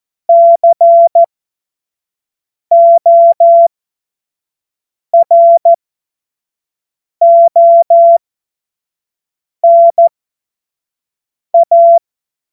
Morse code, heard as a keyed tone, about 14 wpm, spaced as for 5 wpm, 680 Hz. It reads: CORONA
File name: Morsekurs / Witze / Witz 03. August 2021, 25 F70